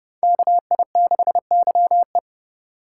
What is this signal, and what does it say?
Morse code, heard as a keyed tone, about 38 wpm, 700 Hz. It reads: KI6YE
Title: Speed Racing - USA 2x2 Call Signs 30wpm